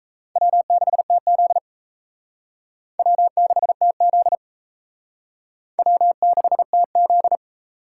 Morse code, heard as a keyed tone, about 35 wpm, 700 Hz. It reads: W6TZ W6TZ W6TZ